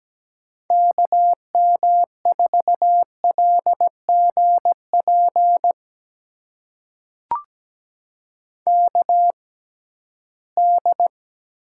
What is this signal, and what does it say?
Morse code, heard as a keyed tone, about 17 wpm, spaced as for 13 wpm, 700 Hz. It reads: KM4LGP KD
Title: USA Call Signs - 6 Characters - 6x Character Spacing - 17wpm